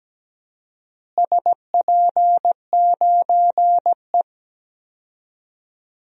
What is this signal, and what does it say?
Morse code, heard as a keyed tone, about 17 wpm, 700 Hz. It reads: SP9E
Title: DXCC-100 Call Signs - 4 Characters - Repeated 3x - 17wpm